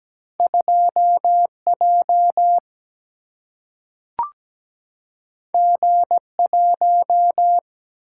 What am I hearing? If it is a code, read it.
Morse code, heard as a keyed tone, about 17 wpm, 700 Hz. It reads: 2J G1
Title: Single-Letter Single-Number 17wpm